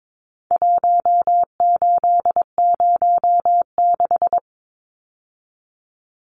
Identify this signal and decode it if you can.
Morse code, heard as a keyed tone, about 22 wpm, 700 Hz. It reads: 1806
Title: Numbers - 4-Digit 22wpm